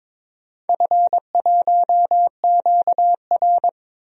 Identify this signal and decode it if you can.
Morse code, heard as a keyed tone, about 21 wpm, 700 Hz. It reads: F1QR
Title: DXCC-100 Call Signs - 4 Characters - 22wpm